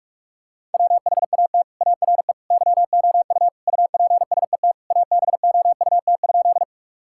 Morse code, heard as a keyed tone, about 42 wpm, 700 Hz. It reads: WHAT ARE YOU UPSET ABOUT?
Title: Sentences from Top 1000 Words - Review Part 5 - 45wpm